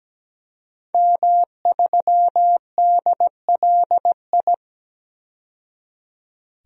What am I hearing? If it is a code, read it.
Morse code, heard as a keyed tone, about 17 wpm, 700 Hz. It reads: M3DLI
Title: DXCC-100 Call Signs - 5 Characters - 17wpm